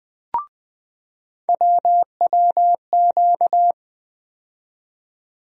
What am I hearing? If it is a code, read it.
Morse code, heard as a keyed tone, about 20 wpm, 700 Hz. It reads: WWQ
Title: USA Call Sign Postfixes - 3 Letters 20wpm